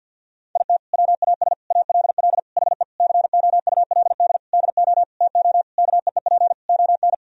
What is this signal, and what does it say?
Morse code, heard as a keyed tone, about 48 wpm, 700 Hz. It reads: IT WAS ALL HE COULD DO TO KEEP ON
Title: Sentences from Top 1000 Words - Review Part 1 - 50wpm